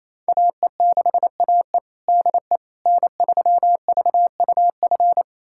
Morse code, heard as a keyed tone, about 28 wpm, 700 Hz. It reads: AE6AE DE N3VUF
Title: USA Callsign DE Callsign - 5 Characters - 28wpm